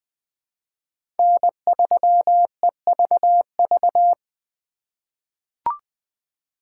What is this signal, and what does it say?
Morse code, heard as a keyed tone, about 20 wpm, 700 Hz. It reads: N3EVV
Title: USA Call Signs - 5 Characters - 4x Character Spacing - 20wpm